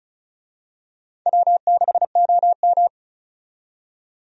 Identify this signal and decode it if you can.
Morse code, heard as a keyed tone, about 36 wpm, 700 Hz. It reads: W6OM
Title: USA Call Signs - 4 Characters - 35wpm